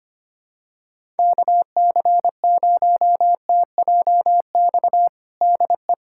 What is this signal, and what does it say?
Morse code, heard as a keyed tone, about 25 wpm, 700 Hz. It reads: KC0TJX DE
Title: USA Callsign DE Callsign 25wpm